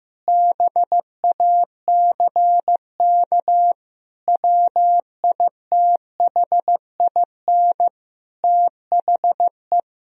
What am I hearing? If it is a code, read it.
Morse code, heard as a keyed tone, about 15 wpm, 700 Hz. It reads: BACK WITHIN THE
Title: Sentences from Top 1000 Words - Review Part 4 - 15wpm